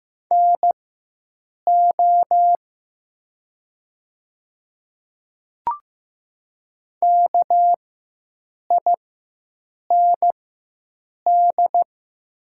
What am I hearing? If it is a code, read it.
Morse code, heard as a keyed tone, about 15 wpm, spaced as for 7 wpm, 700 Hz. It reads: NO KIND
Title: 4x Character Spacing - Top 200 Words - 15wpm